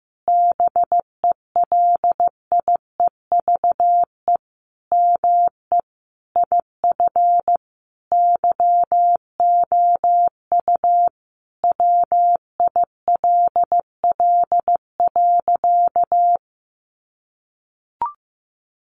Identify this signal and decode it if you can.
Morse code, heard as a keyed tone, about 15 wpm, 700 Hz. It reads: BELIEVE ME IF YOU WILL.
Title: Sentences from Top 1000 Words 15wpm